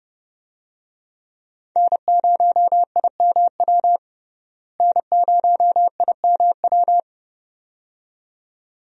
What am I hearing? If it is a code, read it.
Morse code, heard as a keyed tone, about 29 wpm, 700 Hz. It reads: N0IMW N0IMW
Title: USA Call Signs - 5 Characters - Repeated 2x - 30wpm